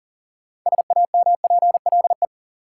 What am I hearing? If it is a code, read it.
Morse code, heard as a keyed tone, about 40 wpm, 700 Hz. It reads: SAMPLE
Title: Six Letter Words - 40wpm